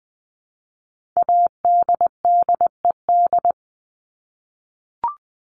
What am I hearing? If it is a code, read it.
Morse code, heard as a keyed tone, about 20 wpm, 700 Hz. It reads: ADDED
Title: ED Suffix - Top 100 Words with Base Words 20wpm